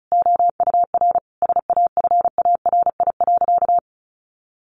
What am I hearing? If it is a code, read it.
Morse code, heard as a keyed tone, about 34 wpm, 700 Hz. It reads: OUR SAFARI.
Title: A Sound Of Thunder 35wpm